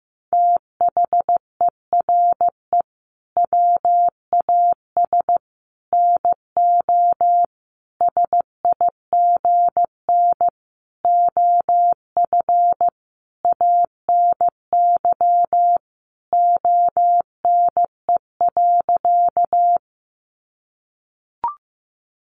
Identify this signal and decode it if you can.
Morse code, heard as a keyed tone, about 15 wpm, 700 Hz. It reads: THERE WAS NO SIGN OF ANY ONE.
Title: Sentences from Top 1000 Words 15wpm